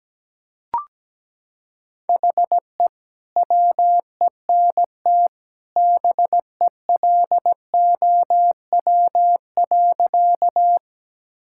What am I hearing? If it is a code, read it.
Morse code, heard as a keyed tone, about 17 wpm, 700 Hz. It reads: HE WENT BELOW.
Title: Sentences from Top 1000 Words - Review Part 5 - 17wpm